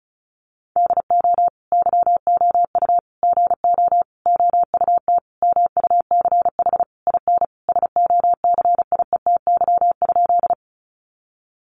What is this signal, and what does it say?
Morse code, heard as a keyed tone, about 34 wpm, 700 Hz. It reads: DO YOU GO OUT MUCH IN SOCIETY?
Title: Sentences from Top 700 Words 35wpm